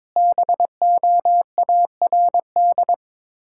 Morse code, heard as a keyed tone, about 23 wpm, 700 Hz. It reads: BOARD
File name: Easy Words - Speed-Racing with Hints - 22wpm